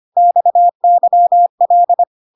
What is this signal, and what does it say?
Morse code, heard as a keyed tone, about 25 wpm, 700 Hz. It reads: XYL
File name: cw / CW common abbreviations alphabetical repeated 3 times 25 25wpm ik1hge